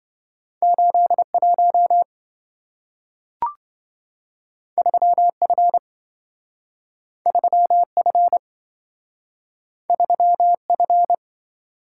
Morse code, heard as a keyed tone, about 27 wpm, 700 Hz. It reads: 81 3F 3F 3F